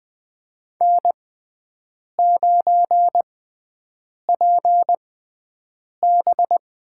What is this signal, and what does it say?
Morse code, heard as a keyed tone, about 21 wpm, spaced as for 7 wpm, 700 Hz. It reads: N9PB
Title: USA Call Signs - 5 Characters - 6x Character Spacing - 20wpm